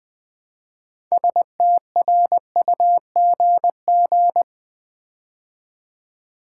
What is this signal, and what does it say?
Morse code, heard as a keyed tone, about 20 wpm, 700 Hz. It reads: STRUGG
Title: Build A Word - 3 Letters At A Time - 8-Letter Words - 20wpm